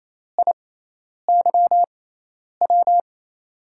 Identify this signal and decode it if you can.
Morse code, heard as a keyed tone, about 28 wpm, spaced as for 10 wpm, 700 Hz. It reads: IYW
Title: USA Call Signs - 6 Characters - 6x Character Spacing - 28wpm